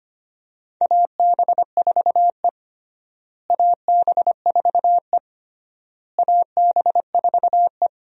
Morse code, heard as a keyed tone, about 25 wpm, 700 Hz. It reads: AB4E AB4E AB4E
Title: USA Call Signs - 4 Characters - Repeated 3x - 25wpm